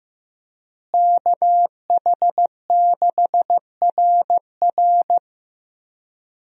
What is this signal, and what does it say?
Morse code, heard as a keyed tone, about 15 wpm, 700 Hz. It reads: KH6RR
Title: USA Call Signs - 5 Characters - Repeated 2x - 15wpm